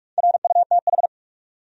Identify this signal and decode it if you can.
Morse code, heard as a keyed tone, about 46 wpm, 700 Hz. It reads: RUTH
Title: USA Common Names - Max 4 Letters - 45wpm